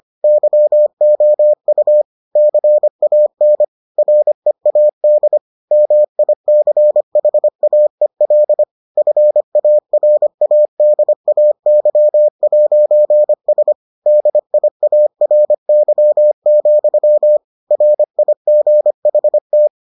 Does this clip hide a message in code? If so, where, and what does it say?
Yes: Morse code from 0.2 to 19.7 s, heard as a keyed tone, about 25 wpm, 600 Hz. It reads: YOU CAN READ MICHAEL FARADAY'S DIARY, RIGHT